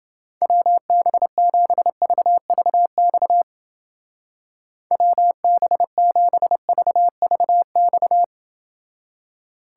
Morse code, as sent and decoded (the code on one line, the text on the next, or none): .-- -... --... ...- ...- -..- / .-- -... --... ...- ...- -..-
WB7VVX WB7VVX